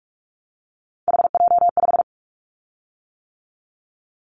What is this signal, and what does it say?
Morse code, heard as a keyed tone, about 46 wpm, 700 Hz. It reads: HJ5
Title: Callsign Prefixes for Top 100 DXCC - 45wpm